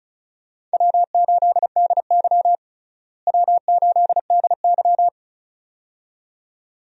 Morse code, heard as a keyed tone, about 36 wpm, 700 Hz. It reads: W8DY W8DY
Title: USA Call Signs - 4 Characters - Repeated 2x - 35wpm